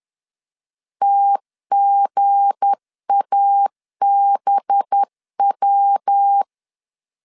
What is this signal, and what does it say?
Morse code, heard as a keyed tone, about 10 wpm, 800 Hz. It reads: TGABW
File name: LEZIONI / 7 GF9Z / 7 GF 10wpm 2514